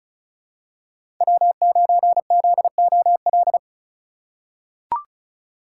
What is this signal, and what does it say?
Morse code, heard as a keyed tone, about 36 wpm, 700 Hz. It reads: W9ZOL
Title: USA Call Signs - 5 Characters - 3x Character Spacing - 35wpm